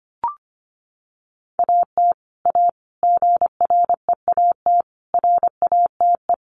Morse code, heard as a keyed tone, about 25 wpm, 700 Hz. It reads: AT A GREAT RATE
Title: Sentences from Top 1200 Words 25wpm